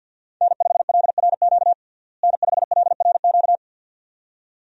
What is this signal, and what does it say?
Morse code, heard as a keyed tone, about 51 wpm, 700 Hz. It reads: N5LRQ N5LRQ
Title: USA Call Signs - 5 Characters - Repeated 2x - 50wpm